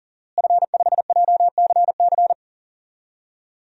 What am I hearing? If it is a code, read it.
Morse code, heard as a keyed tone, about 40 wpm, 700 Hz. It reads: F5JCC
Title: DXCC-100 Call Signs - 5 Characters - Repeated 3x - 40wpm